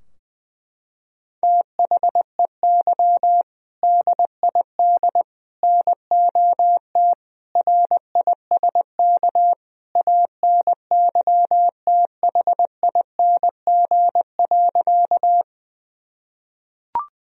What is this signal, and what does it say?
Morse code, heard as a keyed tone, about 20 wpm, 700 Hz. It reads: THEY DID NOT RISK ANYTHING.
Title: Sentences from Top 1000 Words - Review Part 4 - 20wpm